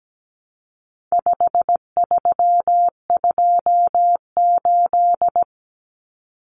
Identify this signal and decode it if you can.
Morse code, heard as a keyed tone, about 17 wpm, 700 Hz. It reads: 5328
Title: Numbers - 4-Digit 17wpm